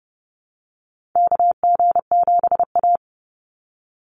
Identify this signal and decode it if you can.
Morse code, heard as a keyed tone, about 30 wpm, 700 Hz. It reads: KG7A
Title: Call Signs - 30wpm